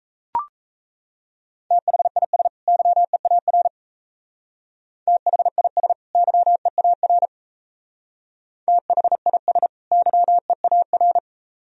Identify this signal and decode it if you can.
Morse code, heard as a keyed tone, about 34 wpm, 700 Hz. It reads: THIS YEAR THIS YEAR THIS YEAR
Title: Speed-Racing - Sets of 2 Words 28wpm